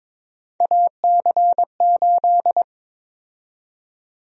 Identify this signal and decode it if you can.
Morse code, heard as a keyed tone, about 22 wpm, 700 Hz. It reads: AC8
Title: USA Call Sign Prefixes - Repeated 2x - 22wpm